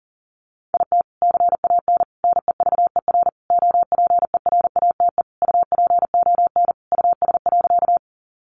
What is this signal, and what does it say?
Morse code, heard as a keyed tone, about 40 wpm, 700 Hz. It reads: IT CAN NEVER OPERATE UPON US.
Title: Sentences from Top 1500 Words 40wpm